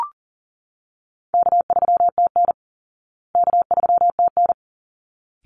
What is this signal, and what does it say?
Morse code, heard as a keyed tone, about 40 wpm, 700 Hz. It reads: K3TN K3TN
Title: Easy Call Signs Repeated 2x - 40wpm